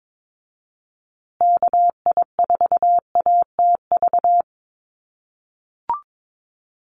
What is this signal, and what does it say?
Morse code, heard as a keyed tone, about 22 wpm, 700 Hz. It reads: KI4ATV
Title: USA Call Signs Repeated 3x 22wpm